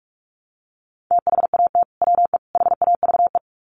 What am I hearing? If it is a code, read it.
Morse code, heard as a keyed tone, about 44 wpm, 700 Hz. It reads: THAT WE HAVE
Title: Sets of 3 Words 45wpm